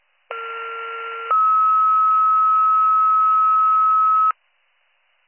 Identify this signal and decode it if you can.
Two-tone page, heard as a one-second tone, then a three-second tone, about 1370 then 1290 Hz. It is heard from 0.3 to 4.3 s.